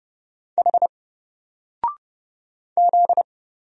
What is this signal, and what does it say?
Morse code, heard as a keyed tone, about 30 wpm, 700 Hz. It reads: H Z
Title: Single Letter-Number Alt 30wpm